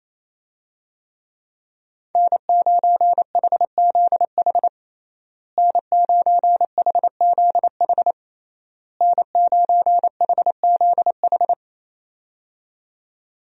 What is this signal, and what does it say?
Morse code, heard as a keyed tone, about 27 wpm, 700 Hz. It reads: N9HZH N9HZH N9HZH
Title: USA Call Signs - 5 Characters - Repeated 3x - 28wpm